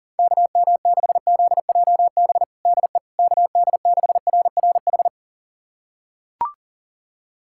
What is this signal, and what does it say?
Morse code, heard as a keyed tone, about 37 wpm, 700 Hz. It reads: KM6ZJB DE KD6RRH
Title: USA Callsign DE Callsign 40wpm